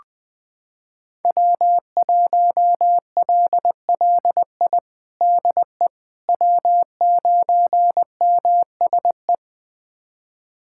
Morse code, heard as a keyed tone, about 20 wpm, 700 Hz. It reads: W1LLI DE W9MSE
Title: USA Callsign DE Callsign - 5 Characters - 20wpm